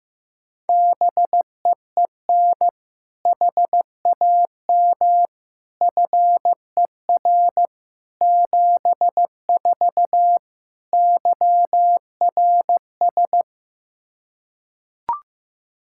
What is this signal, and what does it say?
Morse code, heard as a keyed tone, about 15 wpm, 700 Hz. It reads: BEEN HAM FER 74 YRS
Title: K1USN SST QSO Practice 15wpm